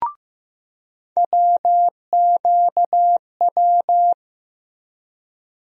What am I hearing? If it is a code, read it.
Morse code, heard as a keyed tone, about 15 wpm, 700 Hz. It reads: WQW
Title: USA Call Sign Postfixes - 3 Letters 15wpm